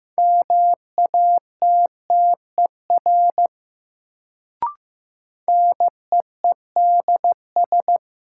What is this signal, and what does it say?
Morse code, heard as a keyed tone, about 15 wpm, 700 Hz. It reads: MATTER NEEDS
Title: Sets of 3 Words - New Word Drill - 15wpm